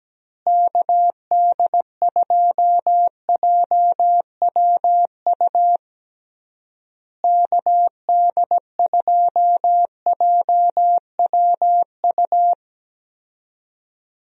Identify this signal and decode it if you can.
Morse code, heard as a keyed tone, about 17 wpm, 700 Hz. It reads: KD2JWU KD2JWU
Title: USA Call Signs - 6 Characters - Repeated 3x - 17wpm